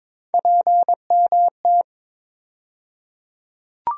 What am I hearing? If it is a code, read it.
Morse code, heard as a keyed tone, about 22 wpm, 700 Hz. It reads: PMT
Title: USA Call Sign Postfixes - 3 Letters 22wpm